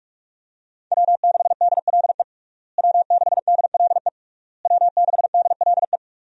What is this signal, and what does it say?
Morse code, heard as a keyed tone, about 46 wpm, 700 Hz. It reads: W6DLE W6DLE W6DLE
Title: USA Call Signs - 5 Characters - Repeated 3x - 45wpm